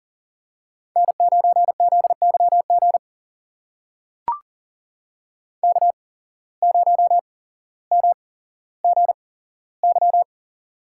Morse code, heard as a keyed tone, about 42 wpm, spaced as for 17 wpm, 700 Hz. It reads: N9ZYG K0MGY
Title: USA Call Signs - 5 Characters - 8x Character Spacing - 40wpm